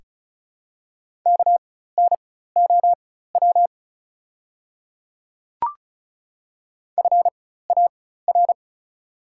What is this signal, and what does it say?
Morse code, heard as a keyed tone, about 36 wpm, spaced as for 16 wpm, 700 Hz. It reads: KNOW FAR
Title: 4x Character Spacing - Top 300 Words - 35wpm